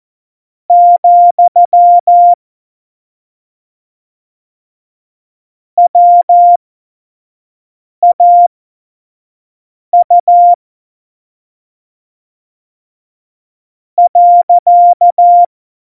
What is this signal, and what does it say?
Morse code, heard as a keyed tone, about 14 wpm, spaced as for 5 wpm, 700 Hz. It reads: , WAU .
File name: Morsekurs / Witze / Witz 31.Juli 2018, 25 F70